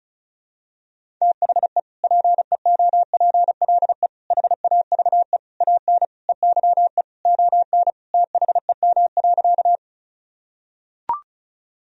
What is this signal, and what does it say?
Morse code, heard as a keyed tone, about 34 wpm, 700 Hz. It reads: THE PEOPLE HAVE AN EYE ON THEM.
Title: Sentences from Top 1000 Words - Review Part 1 - 35wpm